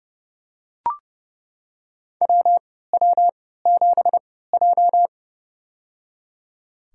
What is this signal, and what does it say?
Morse code, heard as a keyed tone, about 30 wpm, spaced as for 17 wpm, 700 Hz. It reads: WW7J
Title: USA Call Signs - 4 Characters - 3x Character Spacing - 30wpm